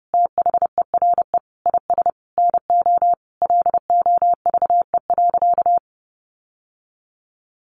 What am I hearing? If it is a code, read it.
Morse code, heard as a keyed tone, about 30 wpm, 700 Hz. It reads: THERE IS NO LOVE.
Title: Sentences from Top 1200 Words 30wpm